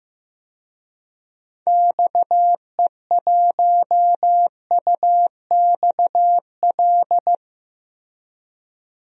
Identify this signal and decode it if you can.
Morse code, heard as a keyed tone, about 15 wpm, 700 Hz. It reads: XE1UXL
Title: DXCC-100 Call Signs - 6 Characters - Repeated 2x - 15wpm